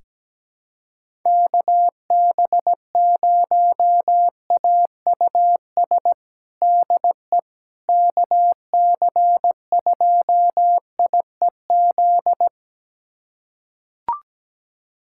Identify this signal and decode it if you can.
Morse code, heard as a keyed tone, about 17 wpm, 700 Hz. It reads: KB0AUS DE KC2IEZ
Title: USA Callsign DE Callsign 17wpm